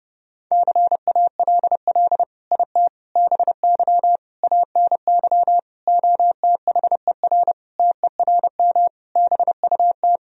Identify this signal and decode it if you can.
Morse code, heard as a keyed tone, about 32 wpm, 700 Hz. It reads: CALL IT BY ANY OTHER TERM BUT
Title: Sentences from Top 1000 Words - Review Part 4 - 30wpm